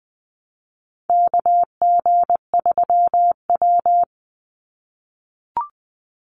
Morse code, heard as a keyed tone, about 20 wpm, 690 Hz. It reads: KG3W
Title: NA SOTA Chaser Callsigns 20wpm